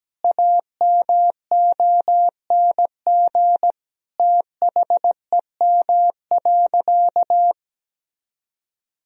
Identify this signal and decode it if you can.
Morse code, heard as a keyed tone, about 17 wpm, 700 Hz. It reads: AMONG THEM.
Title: Sentences from Top 1000 Words - Review Part 3 - 17wpm